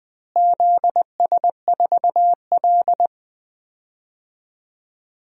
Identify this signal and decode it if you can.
Morse code, heard as a keyed tone, about 20 wpm, 700 Hz. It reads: ZS4L
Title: DXCC-100 Call Signs - 4 Characters - Repeated 2x - 20wpm